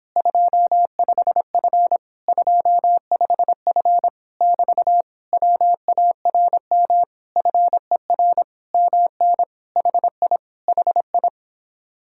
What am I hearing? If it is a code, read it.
Morse code, heard as a keyed tone, about 26 wpm, 700 Hz. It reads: 25F 25F <BT> WARM FER MN HI HI